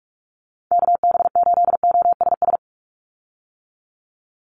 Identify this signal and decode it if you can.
Morse code, heard as a keyed tone, about 46 wpm, 690 Hz. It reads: KB8OSS
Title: USA Call Signs 45wpm